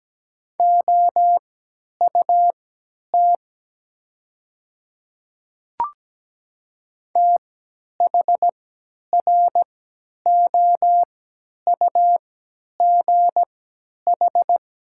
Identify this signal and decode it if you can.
Morse code, heard as a keyed tone, about 17 wpm, spaced as for 10 wpm, 700 Hz. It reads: OUT THROUGH
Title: 3x Character Spacing - Top 200 Words - 17wpm